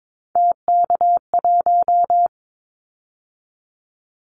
Morse code, heard as a keyed tone, about 22 wpm, 700 Hz. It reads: TK1
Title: Callsign Prefixes for Top 100 DXCC - 22wpm